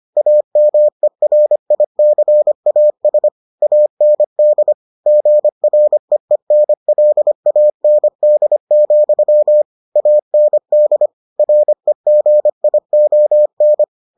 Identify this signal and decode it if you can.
Morse code, heard as a keyed tone, about 25 wpm, 600 Hz. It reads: AMERICAS AND GREENLAND, AND REGION